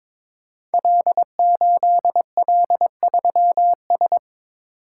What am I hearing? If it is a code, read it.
Morse code, heard as a keyed tone, about 22 wpm, 700 Hz. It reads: L8L3S
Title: Code Groups - 5 Letter:Numbers - 22wpm